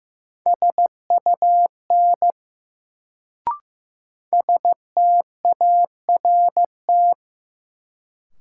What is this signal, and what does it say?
Morse code, heard as a keyed tone, about 15 wpm, 700 Hz. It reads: SUN START
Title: Sentences from Top 300 Words - New Word Drill - 15wpm